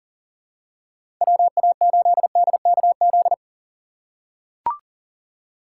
Morse code, heard as a keyed tone, about 37 wpm, 700 Hz. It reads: WA8DKZ
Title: USA Call Signs - 6 Characters - 4x Character Spacing - 40wpm